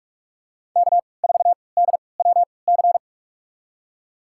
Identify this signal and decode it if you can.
Morse code, heard as a keyed tone, about 44 wpm, spaced as for 25 wpm, 700 Hz. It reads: K4DWC
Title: USA Call Signs - 5 Characters - 3x Character Spacing - 45wpm